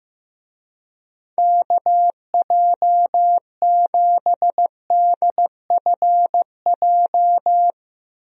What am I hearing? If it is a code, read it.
Morse code, heard as a keyed tone, about 15 wpm, 700 Hz. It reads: KJ7DFJ
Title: USA Call Signs - 6 Characters - Repeated 2x - 15wpm